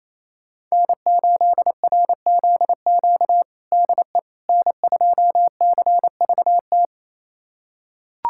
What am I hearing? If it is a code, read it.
Morse code, heard as a keyed tone, about 27 wpm, 700 Hz. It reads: N8RZQ DE N2CVT
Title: USA Callsign DE Callsign - 5 Characters - 28wpm